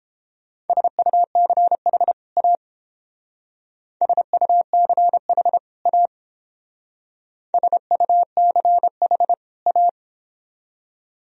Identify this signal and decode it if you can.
Morse code, heard as a keyed tone, about 28 wpm, 700 Hz. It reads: SUCH A SUCH A SUCH A